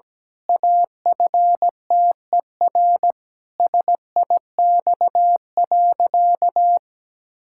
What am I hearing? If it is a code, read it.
Morse code, heard as a keyed tone, about 17 wpm, 700 Hz. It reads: AFTER SIX.
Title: Pangram Challenge v2 - 17wpm